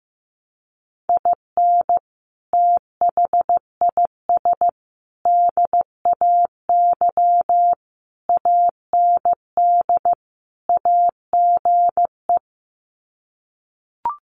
Morse code, heard as a keyed tone, about 15 wpm, 700 Hz. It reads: IN THIS DAY AND AGE
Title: Sets of 5 Words Encore 15wpm